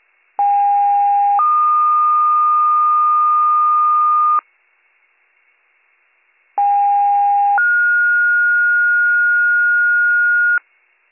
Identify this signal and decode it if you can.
Two-tone page, heard as a one-second tone, then a three-second tone, about 810 then 1220 Hz, about 810 then 1430 Hz. It is heard from 0.4 to 10.6 s.